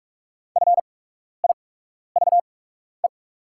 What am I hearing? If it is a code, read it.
Morse code, heard as a keyed tone, about 46 wpm, spaced as for 12 wpm, 700 Hz. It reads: FIVE
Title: 8x Character Spacing - Top 500 Words - 45wpm